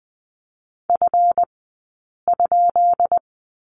Morse code, heard as a keyed tone, about 20 wpm, spaced as for 8 wpm, 700 Hz. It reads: F?
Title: 137 Lesson 23d - Letter f - Two Characters 20wpm